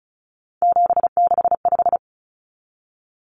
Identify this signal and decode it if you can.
Morse code, heard as a keyed tone, about 34 wpm, 700 Hz. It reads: 765
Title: Numbers - 3-Digit 35wpm